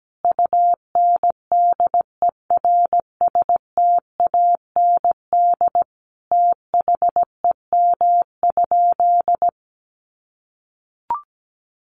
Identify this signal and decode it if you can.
Morse code, heard as a keyed tone, about 17 wpm, 700 Hz. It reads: UNDERSTAND THEM?
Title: Sentences from Top 600 Words 17wpm